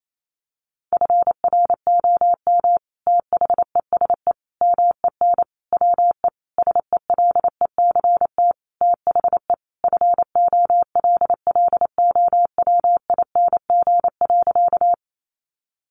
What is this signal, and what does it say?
Morse code, heard as a keyed tone, about 28 wpm, 700 Hz. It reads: FROM THESE MEN WE SELECT THE FOLLOWING.
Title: Sentences from Top 1300 Words 28wpm